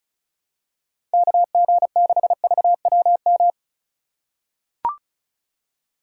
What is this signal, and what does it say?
Morse code, heard as a keyed tone, about 36 wpm, 700 Hz. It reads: KG6VWM
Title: USA Call Signs - 3x Character Spacing - 35wpm